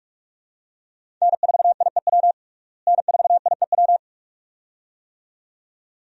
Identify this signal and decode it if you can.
Morse code, heard as a keyed tone, about 43 wpm, 700 Hz. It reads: N4IEW N4IEW
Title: USA Call Signs - 5 Characters - Repeated 2x - 45wpm